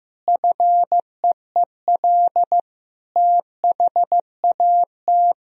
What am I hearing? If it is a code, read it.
Morse code, heard as a keyed tone, about 15 wpm, 700 Hz. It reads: FEEL THAT
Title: Sentences from Top 1000 Words - Review Part 1 - 15wpm